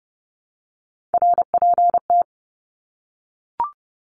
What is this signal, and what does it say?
Morse code, heard as a keyed tone, about 30 wpm, 700 Hz. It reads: RPT
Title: Abbreviations Repeated 3x - 30wpm